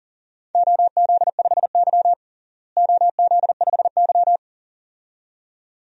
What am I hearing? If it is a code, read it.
Morse code, heard as a keyed tone, about 37 wpm, 700 Hz. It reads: OZ5Y OZ5Y
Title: DXCC-100 Call Signs - 4 Characters - Repeated 2x - 40wpm